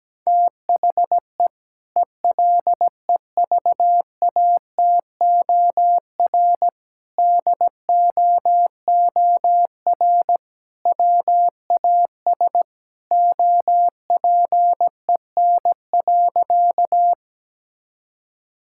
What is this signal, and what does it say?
Morse code, heard as a keyed tone, about 17 wpm, 700 Hz. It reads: THE ELEVATOR DOOR WAS OPEN.
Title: Sentences from Top 1600 Words 17wpm